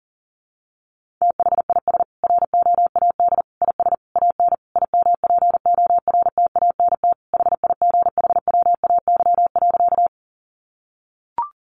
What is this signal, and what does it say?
Morse code, heard as a keyed tone, about 40 wpm, 700 Hz. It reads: THIS ROAD IS AN IMPORTANT HIGHWAY.
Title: Sentences from Top 1400 Words 40wpm